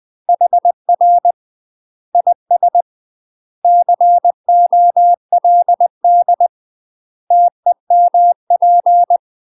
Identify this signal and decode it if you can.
Morse code, heard as a keyed tone, about 20 wpm, 700 Hz. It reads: HR IS COLD TEMP
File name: UK MASTER QSOs TRAINING / 20WPM / UK MASTER QSOs 0044